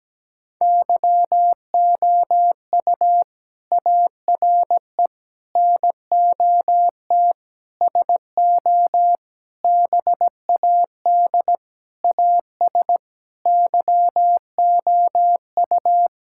Morse code, heard as a keyed tone, about 17 wpm, 700 Hz. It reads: YOU ARE NOT SO BAD AS YOU
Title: Sentences from Top 1000 Words - Review Part 6 - 17wpm